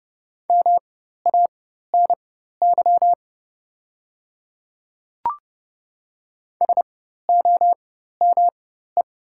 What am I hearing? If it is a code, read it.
Morse code, heard as a keyed tone, about 30 wpm, spaced as for 14 wpm, 690 Hz. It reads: MANY SOME
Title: 4x Character Spacing - Top 100 Words - 30wpm